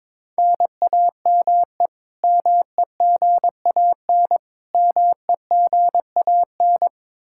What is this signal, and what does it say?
Morse code, heard as a keyed tone, about 21 wpm, 700 Hz. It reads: NAME MEGAN MEGAN
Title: Practice QSO - Exchange 1 - Beginning - 22wpm